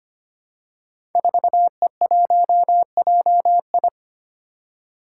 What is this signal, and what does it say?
Morse code, heard as a keyed tone, about 25 wpm, 700 Hz. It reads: 4E1JI
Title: Five Challenging Letter-Number Combinations 25wpm